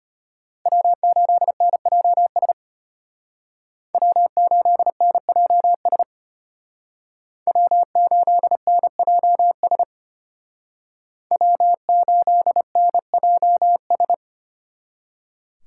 Morse code, heard as a keyed tone, about 32 wpm, 700 Hz. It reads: W8NJS W8NJS W8NJS W8NJS